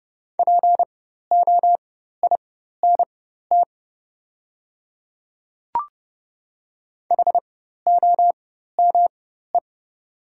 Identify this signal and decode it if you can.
Morse code, heard as a keyed tone, about 31 wpm, spaced as for 14 wpm, 700 Hz. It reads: POINT HOME
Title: 4x Character Spacing - Top 200 Words - 30wpm